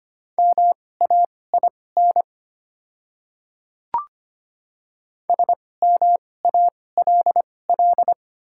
Morse code, heard as a keyed tone, about 25 wpm, spaced as for 18 wpm, 700 Hz. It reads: MAIN SMALL
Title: 2x Character Spacing - Top 500 Words - 25wpm